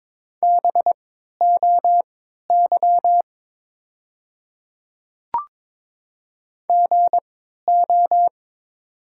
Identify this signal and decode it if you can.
Morse code, heard as a keyed tone, about 22 wpm, spaced as for 13 wpm, 700 Hz. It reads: BOY GO
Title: 3x Character Spacing - Top 200 Words - 22wpm